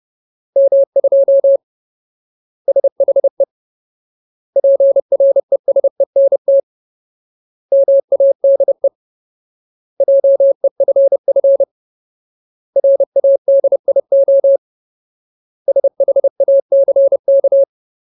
Morse code, heard as a keyed tone, about 30 wpm, 550 Hz. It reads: M2 SHE PRESENT MADE JEFF RADIO SHACK
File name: Ultimate Word List / 30WPM Ultimate